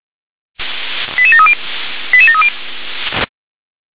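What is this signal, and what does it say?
Five-tone selective call (ZVEI), heard as quick runs of five tones, tones of about 70 ms. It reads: 88730 88730
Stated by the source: ZVEI five-tone call received by SDR.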